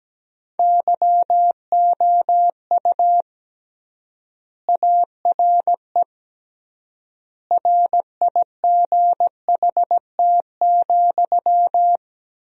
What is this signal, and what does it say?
Morse code, heard as a keyed tone, about 17 wpm, 700 Hz. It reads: YOU ARE RIGHT,
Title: 3x Word-Spacing - Sentences from Top 400 Words - 17wpm